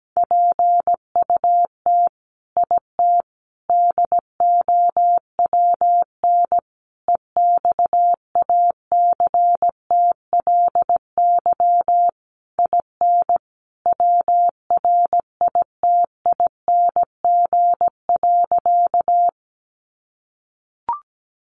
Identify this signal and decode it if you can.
Morse code, heard as a keyed tone, about 17 wpm, 700 Hz. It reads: PUT IT DOWN EXACTLY IN WRITING.
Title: Sentences from Top 900 Words 17wpm